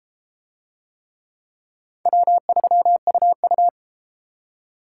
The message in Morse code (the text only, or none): W3UU